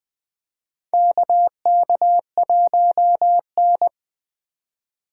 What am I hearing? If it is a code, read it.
Morse code, heard as a keyed tone, about 20 wpm, 700 Hz. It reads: KK1N
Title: USA Call Signs - 4 Characters - 2x Character Spacing - 20wpm